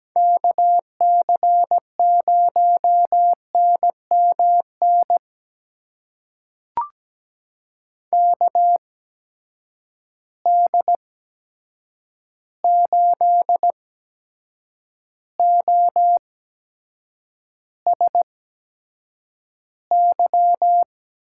Morse code, heard as a keyed tone, about 17 wpm, spaced as for 7 wpm, 700 Hz. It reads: KC0NMN KD8OSY
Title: USA Call Signs - 6 Characters - 8x Character Spacing - 17wpm